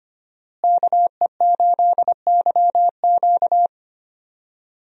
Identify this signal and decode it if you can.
Morse code, heard as a keyed tone, about 25 wpm, 700 Hz. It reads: KE8YQ
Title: USA Call Signs - 5 Characters - Repeated 2x - 25wpm